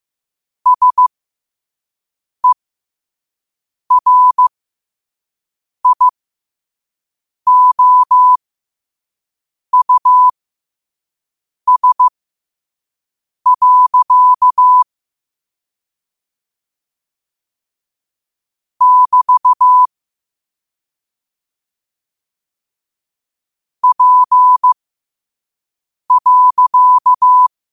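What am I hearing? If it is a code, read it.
Morse code, heard as a keyed tone, about 15 wpm, spaced as for 5 wpm, 1000 Hz. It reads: SERIOUS. <BT> P.
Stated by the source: Morse code sent at 5 wpm.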